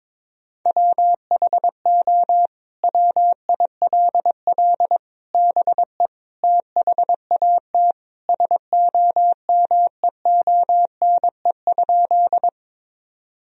Morse code, heard as a keyed tone, about 22 wpm, 700 Hz. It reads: WHO WILL BE THAT SOMEONE?
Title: Sentences from Top 1000 Words - Review Part 4 - 22wpm